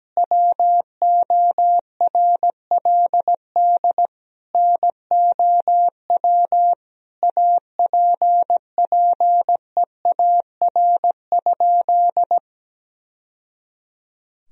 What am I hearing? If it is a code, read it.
Morse code, heard as a keyed tone, about 17 wpm, 700 Hz. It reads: WORLD NOW APPEAR?
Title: Sentences from Top 1000 Words - Review Part 2 - 17wpm